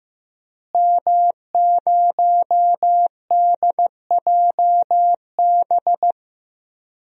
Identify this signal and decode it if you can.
Morse code, heard as a keyed tone, about 15 wpm, 700 Hz. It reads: M0DJB
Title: DXCC-100 Call Signs - 5 Characters - Repeated 3x - 15wpm